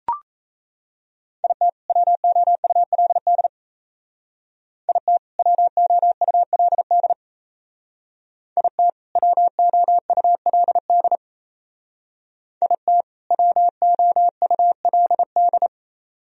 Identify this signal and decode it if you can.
Morse code, heard as a keyed tone, about 36 wpm, 700 Hz. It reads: IT WOULD IT WOULD IT WOULD IT WOULD